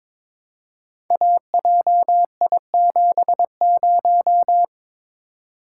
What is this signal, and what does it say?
Morse code, heard as a keyed tone, about 23 wpm, 700 Hz. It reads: AJI70
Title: Code Groups - 5 Letter:Numbers - 22wpm